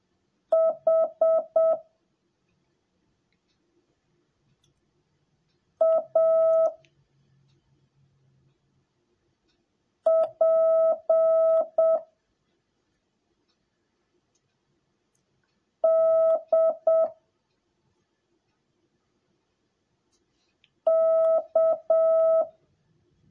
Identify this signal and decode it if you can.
Morse code, heard as a keyed tone, about 7 wpm, 650 Hz. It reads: H A P D K
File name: jh8cbh / 2021cw / cw10a